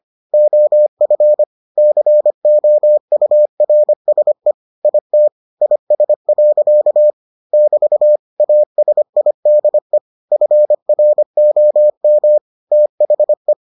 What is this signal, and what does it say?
Morse code, heard as a keyed tone, about 25 wpm, 600 Hz. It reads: OF COURSE IT IS. <BT> ASIDE FROM THE